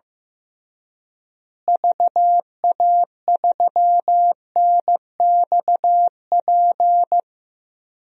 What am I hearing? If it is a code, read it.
Morse code, heard as a keyed tone, about 15 wpm, 700 Hz. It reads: VA3NXP
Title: DXCC-100 Call Signs - 6 Characters - Repeated 2x - 15wpm